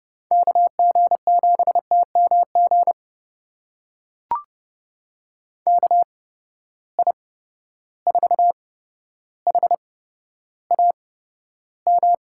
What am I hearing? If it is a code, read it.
Morse code, heard as a keyed tone, about 30 wpm, spaced as for 13 wpm, 700 Hz. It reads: KG7TMG KI4HAM
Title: USA Call Signs - 6 Characters - 8x Character Spacing - 30wpm